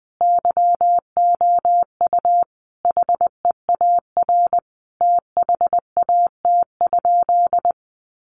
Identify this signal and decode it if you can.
Morse code, heard as a keyed tone, about 20 wpm, 700 Hz. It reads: YOU HEAR THAT?
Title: Sentences from Top 1500 Words 20wpm